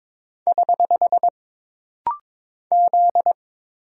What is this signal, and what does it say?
Morse code, heard as a keyed tone, about 22 wpm, 700 Hz. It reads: <HH> Z
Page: This rapid-fire timing is geared to push your ICR (Instant Character Recognition) skills.